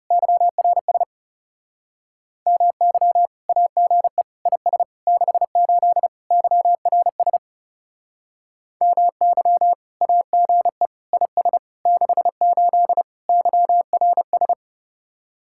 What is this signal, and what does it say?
Morse code, heard as a keyed tone, about 38 wpm, 700 Hz. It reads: YRS MY AGE IS 68 YRS MY AGE IS 68 YRS